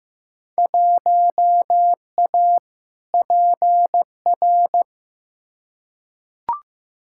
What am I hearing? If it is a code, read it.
Morse code, heard as a keyed tone, about 15 wpm, 700 Hz. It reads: 1A PR
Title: Field Day Exchanges Encore - 15wpm